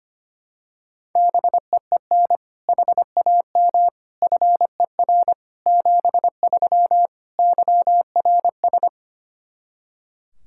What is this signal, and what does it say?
Morse code, heard as a keyed tone, about 25 wpm, 700 Hz. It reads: BEEN HAM FER 73 YRS
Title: Practice QSO - Exchange 3 - More Info - 25wpm